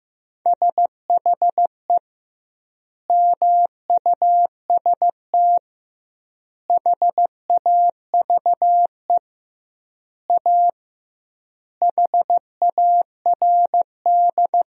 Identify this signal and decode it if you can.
Morse code, heard as a keyed tone, about 15 wpm, 700 Hz. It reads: SHE MUST HAVE A HARD
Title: 2x Word-Spacing - Sentences from Top 300 Words - 15wpm